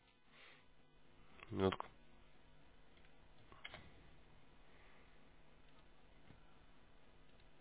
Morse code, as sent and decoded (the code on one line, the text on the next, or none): none